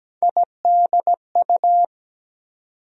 Morse code, heard as a keyed tone, about 17 wpm, 700 Hz. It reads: IDU
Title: USA Call Sign Postfixes - 3 Letters 17wpm